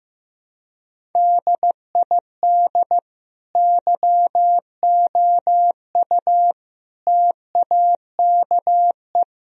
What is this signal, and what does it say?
Morse code, heard as a keyed tone, about 15 wpm, 700 Hz. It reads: DID YOU TAKE
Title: Sentences from Top 1000 Words - Review Part 1 - 15wpm